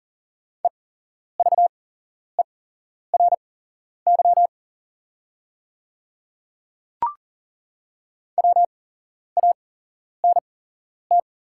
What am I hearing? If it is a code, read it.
Morse code, heard as a keyed tone, about 40 wpm, spaced as for 11 wpm, 690 Hz. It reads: EVERY WANT
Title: 8x Character Spacing - Top 200 Words - 40wpm